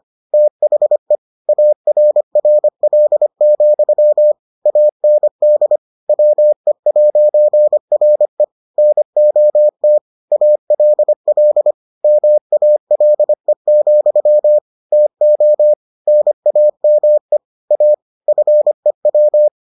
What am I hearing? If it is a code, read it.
Morse code, heard as a keyed tone, about 25 wpm, 600 Hz. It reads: THE ARRL, AND WE'RE NOT ALL MALE, TO NAME A FEW